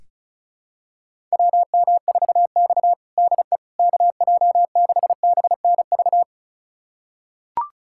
Morse code, heard as a keyed tone, about 34 wpm, 700 Hz. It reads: WM4X DE KJ6BNV
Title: USA Callsign DE Callsign 35wpm